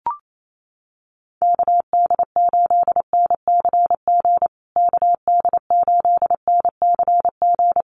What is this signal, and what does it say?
Morse code, heard as a keyed tone, about 28 wpm, 700 Hz. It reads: KD8NCG KD8NCG
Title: USA Call Signs Repeated 3x 28wpm